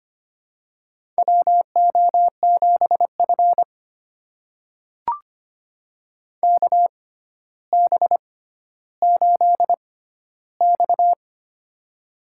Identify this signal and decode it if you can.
Morse code, heard as a keyed tone, about 25 wpm, spaced as for 13 wpm, 700 Hz. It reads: WO7F KB8X
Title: USA Call Signs - 4 Characters - 6x Character Spacing - 25wpm